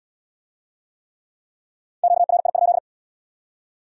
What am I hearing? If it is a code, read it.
Morse code, heard as a keyed tone, about 64 wpm, 700 Hz. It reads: QRI?